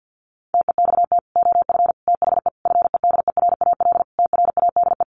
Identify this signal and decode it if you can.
Morse code, heard as a keyed tone, about 49 wpm, 700 Hz. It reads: TEXT OF THE FEDERAL TRADE
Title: Sentences from Top 1400 Words 50wpm